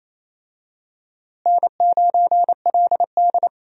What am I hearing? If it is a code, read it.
Morse code, heard as a keyed tone, about 27 wpm, 700 Hz. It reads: N9LD
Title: USA Call Signs - 4 Characters - Repeated 3x - 28wpm